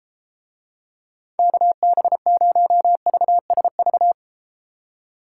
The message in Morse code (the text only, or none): KB0VSV